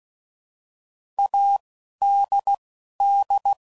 Morse code, heard as a keyed tone, about 16 wpm, spaced as for 11 wpm, 800 Hz. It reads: ADD
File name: Prefixes / By First Letter / As